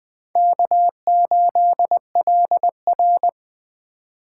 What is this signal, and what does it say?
Morse code, heard as a keyed tone, about 20 wpm, 700 Hz. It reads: K8LR
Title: Speed Racing - USA 1x2 Call Signs 17wpm